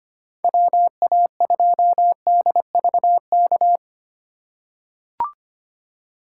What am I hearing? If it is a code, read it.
Morse code, heard as a keyed tone, about 25 wpm, 700 Hz. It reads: WA2DVK
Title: USA Call Signs - 6 Characters - 8x Character Spacing - 25wpm